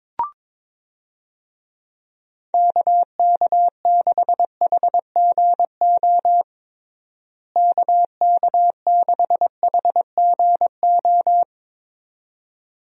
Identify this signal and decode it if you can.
Morse code, heard as a keyed tone, about 21 wpm, 700 Hz. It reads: KK6HGO KK6HGO
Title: USA Call Signs - 6 Characters - Repeated 2x - 22wpm